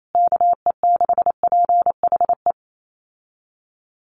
Morse code, heard as a keyed tone, about 28 wpm, 700 Hz. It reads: KE6PHE
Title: USA Call Signs Repeated 3x 28wpm